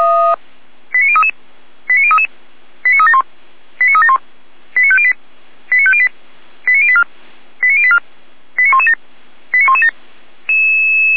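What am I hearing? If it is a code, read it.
Five-tone selective call (ZVEI), heard as quick runs of five tones, tones of about 70 ms. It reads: 89033 89033 89371 89371 89508 89508 89084 89084 89117 89117
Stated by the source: ZVEI five-tone call 89080 received on an SDR.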